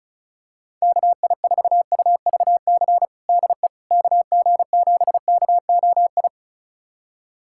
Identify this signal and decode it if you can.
Morse code, heard as a keyed tone, about 34 wpm, 700 Hz. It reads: KI4UVC DE KG7KOI
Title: USA Callsign DE Callsign - 6 Characters - 35wpm